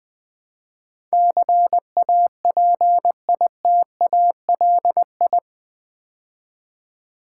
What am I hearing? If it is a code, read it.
Morse code, heard as a keyed tone, about 21 wpm, 700 Hz. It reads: CAPITALI
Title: Build A Word - 4 Letters At A Time - 10-Letter Words - 20wpm